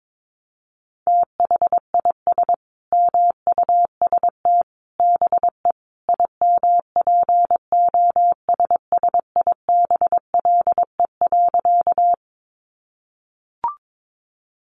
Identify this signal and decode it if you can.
Morse code, heard as a keyed tone, about 22 wpm, 700 Hz. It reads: THIS MUST BE IMPOSSIBLE.
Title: Sentences from Top 1100 Words 22wpm